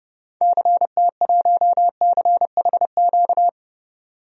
Morse code, heard as a keyed tone, about 30 wpm, 700 Hz. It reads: CT1CHQ
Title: DXCC-100 Call Signs - 6 Characters - 30wpm